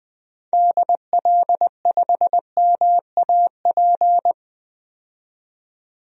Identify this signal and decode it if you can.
Morse code, heard as a keyed tone, about 20 wpm, 700 Hz. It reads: DL5MAP
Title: DXCC-100 Call Signs - 6 Characters - 20wpm